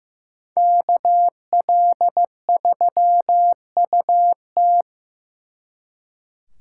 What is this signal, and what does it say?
Morse code, heard as a keyed tone, about 18 wpm, 700 Hz. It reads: KL3UT
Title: Speed Racing - USA Call Signs 15wpm